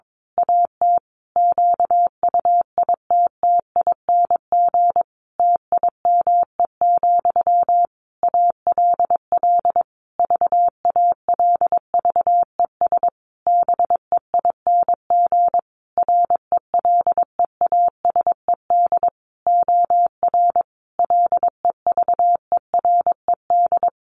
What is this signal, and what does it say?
Morse code, heard as a keyed tone, about 22 wpm, 700 Hz. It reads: AT QUITTING TIME, ALL VALVES BEING RELEASED OR LEVERED
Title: A Sound Of Thunder 22wpm